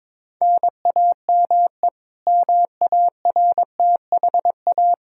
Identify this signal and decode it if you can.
Morse code, heard as a keyed tone, about 22 wpm, 700 Hz. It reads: NAME MARTHA
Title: Practice QSO - Exchange 1 - Beginning - 22wpm